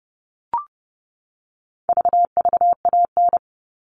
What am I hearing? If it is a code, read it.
Morse code, heard as a keyed tone, about 30 wpm, 700 Hz. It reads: VVAN
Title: All Cut Numbers - 4-Digit 30wpm